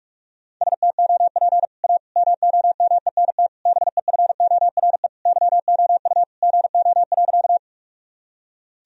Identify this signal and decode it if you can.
Morse code, heard as a keyed tone, about 41 wpm, 700 Hz. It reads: STOP A MOMENT BEFORE YOU GO.
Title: Sentences from Top 1000 Words - Review Part 1 - 45wpm